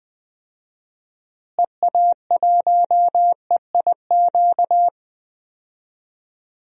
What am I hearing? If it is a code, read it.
Morse code, heard as a keyed tone, about 20 wpm, 700 Hz. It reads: EA1EIQ
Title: DXCC-100 Call Signs - 6 Characters - Repeated 3x - 20wpm